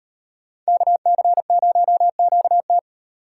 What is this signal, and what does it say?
Morse code, heard as a keyed tone, about 36 wpm, 700 Hz. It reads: KC0QT